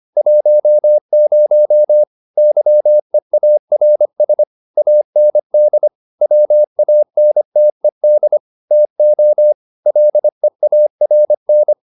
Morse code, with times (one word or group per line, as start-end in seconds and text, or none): 0.2-2.0 10
2.4-4.4 YEARS
4.8-5.9 AND
6.2-8.4 WANTED
8.7-9.5 TO
9.9-11.7 LEARN